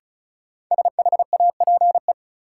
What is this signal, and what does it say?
Morse code, heard as a keyed tone, about 36 wpm, 700 Hz. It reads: SHAPE
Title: Sets of 5 Words Encore - New Word Drill - 35wpm